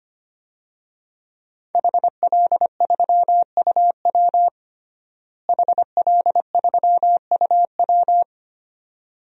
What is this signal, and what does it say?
Morse code, heard as a keyed tone, about 25 wpm, 700 Hz. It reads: HL3UW HL3UW
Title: DXCC-100 Call Signs - 5 Characters - Repeated 2x - 25wpm